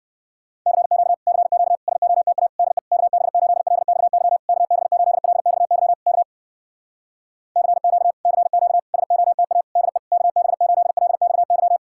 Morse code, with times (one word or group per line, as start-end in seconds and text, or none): 0.7-1.1 CQ
1.3-1.8 CQ
1.9-2.5 SOTA
2.6-2.8 DE
2.9-4.4 KB8LXY
4.5-5.9 KB8LXY
6.1-6.2 K
7.6-8.1 CQ
8.3-8.8 CQ
8.9-9.6 SOTA
9.8-10.0 DE
10.1-11.8 KB8LXY